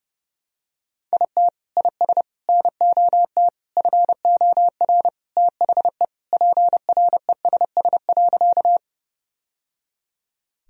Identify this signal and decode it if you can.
Morse code, heard as a keyed tone, about 30 wpm, 700 Hz. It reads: IT IS NOT FOR THE PRESS.
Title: Sentences from Top 1000 Words - Review Part 1 - 30wpm